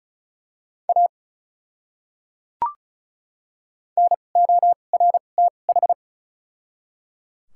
Morse code, heard as a keyed tone, about 36 wpm, spaced as for 25 wpm, 700 Hz. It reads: A NORTH
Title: 2x Character Spacing - Top 300 Words - 35wpm